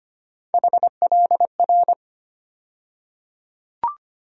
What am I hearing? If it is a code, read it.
Morse code, heard as a keyed tone, about 25 wpm, 700 Hz. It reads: HLR
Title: USA Call Sign Postfixes - 3 Letters 25wpm